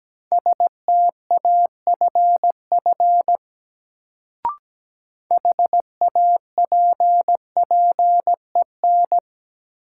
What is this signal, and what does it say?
Morse code, heard as a keyed tone, about 17 wpm, 700 Hz. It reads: STAFF HAPPEN
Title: Sets of 5 Words Encore - New Word Drill - 17wpm